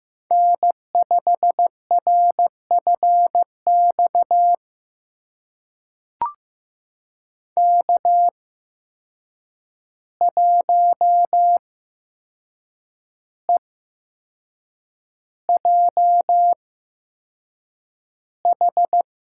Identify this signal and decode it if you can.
Morse code, heard as a keyed tone, about 15 wpm, spaced as for 6 wpm, 700 Hz. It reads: N5RFX K1EJH
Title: USA Call Signs - 5 Characters - 8x Character Spacing - 15wpm